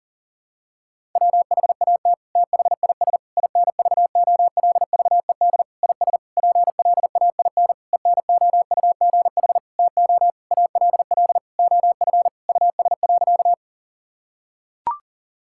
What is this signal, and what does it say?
Morse code, heard as a keyed tone, about 40 wpm, 700 Hz. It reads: WHAT THIS INVOLVED IS PLAIN ENOUGH TO ALL OF US.
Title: Sentences from Top 1000 Words - Review Part 5 - 40wpm